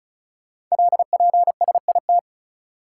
Morse code, heard as a keyed tone, about 36 wpm, 690 Hz. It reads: LPSIT
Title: Code Groups - 5 Letters - 35wpm